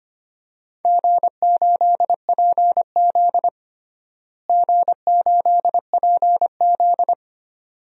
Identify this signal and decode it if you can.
Morse code, heard as a keyed tone, about 25 wpm, 700 Hz. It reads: G8PZ G8PZ
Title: DXCC-100 Call Signs - 4 Characters - Repeated 2x - 25wpm